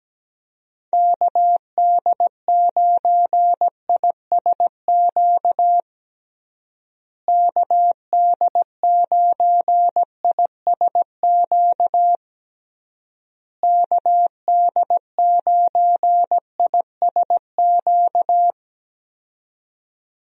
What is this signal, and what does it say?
Morse code, heard as a keyed tone, about 17 wpm, 700 Hz. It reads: KD9ISQ KD9ISQ KD9ISQ
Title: USA Call Signs - 6 Characters - Repeated 3x - 17wpm